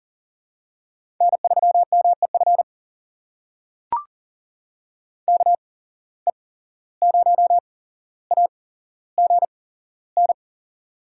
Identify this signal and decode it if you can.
Morse code, heard as a keyed tone, about 40 wpm, spaced as for 16 wpm, 700 Hz. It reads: N3MEF KE0AGN
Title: USA Call Signs - 8x Character Spacing - 40wpm